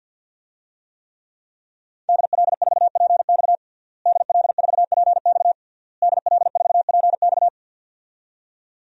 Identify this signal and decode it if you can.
Morse code, heard as a keyed tone, about 53 wpm, 700 Hz. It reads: DL4PX DL4PX DL4PX
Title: DXCC-100 Call Signs - 5 Characters - Repeated 3x - 50wpm